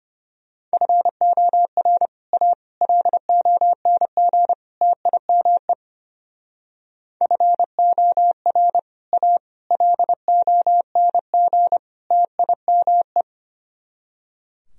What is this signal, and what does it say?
Morse code, heard as a keyed tone, about 31 wpm, 700 Hz. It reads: FOR A LONG TIME FOR A LONG TIME
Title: Speed-Racing - Sets of 4 Words 25wpm